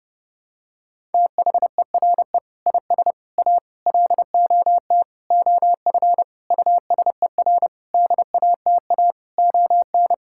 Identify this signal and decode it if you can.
Morse code, heard as a keyed tone, about 29 wpm, 700 Hz. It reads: THERE IS A LOT OF USER DATA ON
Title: Sentences from Top 1000 Words - Review Part 6 - 30wpm